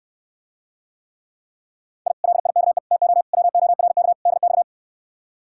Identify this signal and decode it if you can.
Morse code, heard as a keyed tone, about 67 wpm, 700 Hz. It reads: I LIKE TO WORK DX